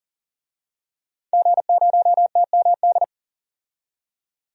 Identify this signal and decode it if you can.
Morse code, heard as a keyed tone, about 40 wpm, 700 Hz. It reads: G0TMD
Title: DXCC-100 Call Signs - 5 Characters - 40wpm